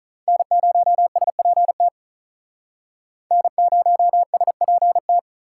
Morse code, heard as a keyed tone, about 45 wpm, 700 Hz. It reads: N0SPT N0SPT